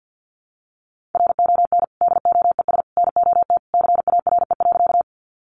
Morse code, heard as a keyed tone, about 48 wpm, 700 Hz. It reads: RON DOES NOT CARE.
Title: Additional Easy Phrases 50wpm